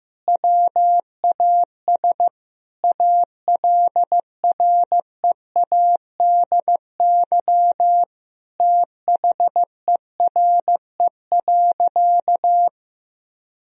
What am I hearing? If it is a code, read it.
Morse code, heard as a keyed tone, about 15 wpm, 700 Hz. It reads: WAS ALREADY THERE.
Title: Sentences from Top 1000 Words - Review Part 1 - 15wpm